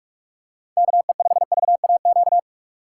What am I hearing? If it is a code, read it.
Morse code, heard as a keyed tone, about 45 wpm, 690 Hz. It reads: KE5VAQ
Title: USA Call Signs - 8x Character Spacing - 45wpm